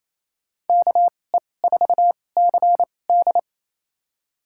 Morse code, heard as a keyed tone, about 28 wpm, spaced as for 20 wpm, 700 Hz. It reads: KE4CD
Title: USA Call Signs - 2x Character Spacing - 28wpm